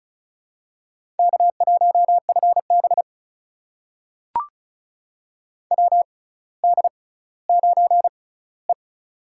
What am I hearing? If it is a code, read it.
Morse code, heard as a keyed tone, about 36 wpm, spaced as for 18 wpm, 700 Hz. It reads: K1FB WD9E
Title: USA Call Signs - 4 Characters - 6x Character Spacing - 35wpm